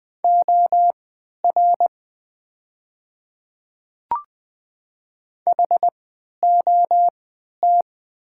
Morse code, heard as a keyed tone, about 20 wpm, spaced as for 11 wpm, 700 Hz. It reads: OR HOT
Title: 3x Character Spacing - Top 100 Words - 20wpm